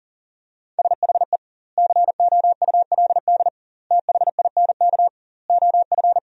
Morse code, heard as a keyed tone, about 40 wpm, 700 Hz. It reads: SHE COULD THINK OF
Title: 2x Word-Spacing - Sentences from Top 500 Words - 40wpm